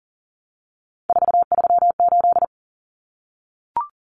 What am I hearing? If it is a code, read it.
Morse code, heard as a keyed tone, about 40 wpm, 700 Hz. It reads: 438
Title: Common Cut Numbers - 3-Digit 40wpm